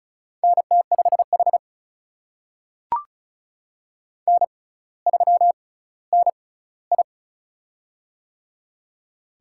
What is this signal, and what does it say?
Morse code, heard as a keyed tone, about 36 wpm, spaced as for 18 wpm, 700 Hz. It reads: NT5H N3NI
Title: USA Call Signs - 4 Characters - 6x Character Spacing - 35wpm